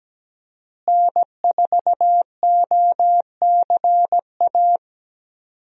Morse code, heard as a keyed tone, about 17 wpm, 700 Hz. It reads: N4OCA
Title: USA Call Signs - 5 Characters - Repeated 3x - 17wpm